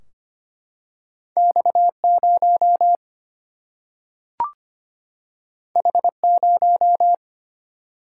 Morse code, heard as a keyed tone, about 25 wpm, 700 Hz. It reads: X0 H0
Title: Single-Letter Single-Number 25wpm